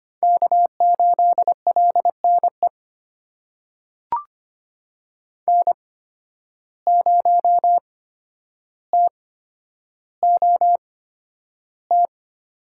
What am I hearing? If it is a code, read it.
Morse code, heard as a keyed tone, about 25 wpm, spaced as for 11 wpm, 700 Hz. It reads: K8LNE N0TOT
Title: USA Call Signs - 5 Characters - 8x Character Spacing - 25wpm